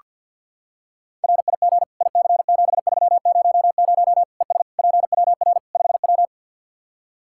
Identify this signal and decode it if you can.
Morse code, heard as a keyed tone, about 47 wpm, 700 Hz. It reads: RIG IC7300 ES PWR 5W
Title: Practice QSO - Exchange 2 - Info - 50wpm